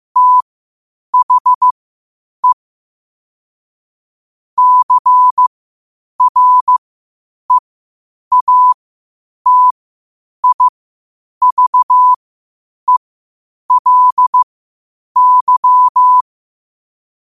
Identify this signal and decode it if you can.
Morse code, heard as a keyed tone, about 15 wpm, spaced as for 8 wpm, 1000 Hz. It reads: THE CREATIVELY
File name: qotdfiles / 8wpm